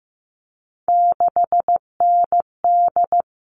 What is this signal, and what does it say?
Morse code, heard as a keyed tone, about 15 wpm, 700 Hz. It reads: 6ND
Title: All Cut Numbers - 3-Digit 15wpm